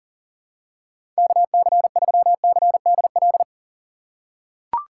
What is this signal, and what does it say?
Morse code, heard as a keyed tone, about 40 wpm, 700 Hz. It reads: KC3CDL
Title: USA Call Signs - 6 Characters - 3x Character Spacing - 40wpm